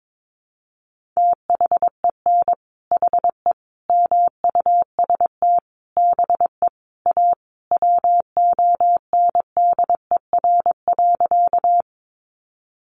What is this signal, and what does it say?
Morse code, heard as a keyed tone, about 22 wpm, 700 Hz. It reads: THEN HE MUST BE A WONDER.
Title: Sentences from Top 500 Words 22wpm